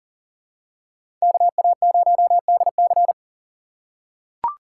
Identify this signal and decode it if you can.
Morse code, heard as a keyed tone, about 38 wpm, 700 Hz. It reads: KA0DC
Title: USA Call Signs - 5 Characters - 3x Character Spacing - 40wpm